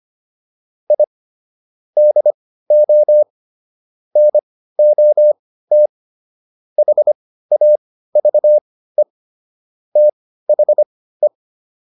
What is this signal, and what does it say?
Morse code, heard as a keyed tone, about 25 wpm, spaced as for 15 wpm, 600 Hz. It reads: I DO NOT HAVE THE